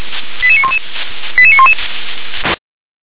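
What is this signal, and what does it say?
Five-tone selective call (ZVEI), heard as quick runs of five tones, tones of about 70 ms. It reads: 88911 88911
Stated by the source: ZVEI five-tone call received via SDR.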